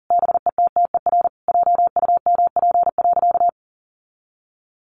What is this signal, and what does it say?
Morse code, heard as a keyed tone, about 40 wpm, 700 Hz. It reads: BETTER JUMP.
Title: Sentences from Top 1300 Words 40wpm